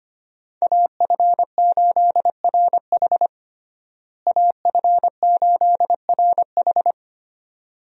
Morse code, heard as a keyed tone, about 25 wpm, 700 Hz. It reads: AF8RH AF8RH
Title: USA Call Signs - 5 Characters - Repeated 2x - 25wpm